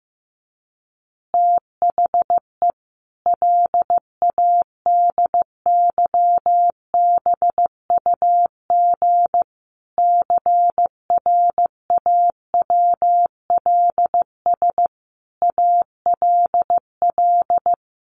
Morse code, heard as a keyed tone, about 15 wpm, 700 Hz. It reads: THE LADYBUG CRAWLS ALL
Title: Additional Easy Phrases 15wpm